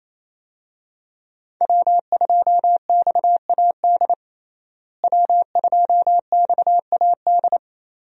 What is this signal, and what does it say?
Morse code, heard as a keyed tone, about 28 wpm, 700 Hz. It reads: W2XAD W2XAD
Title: USA Call Signs - 5 Characters - Repeated 3x - 28wpm